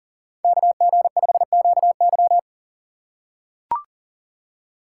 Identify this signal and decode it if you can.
Morse code, heard as a keyed tone, about 37 wpm, 700 Hz. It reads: KG5QY
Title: USA Call Signs - 5 Characters - 4x Character Spacing - 40wpm